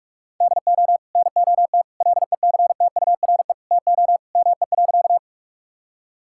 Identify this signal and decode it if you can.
Morse code, heard as a keyed tone, about 46 wpm, 700 Hz. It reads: DO NOT LECTURE TO ME.
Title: Sentences from Top 1600 Words 45wpm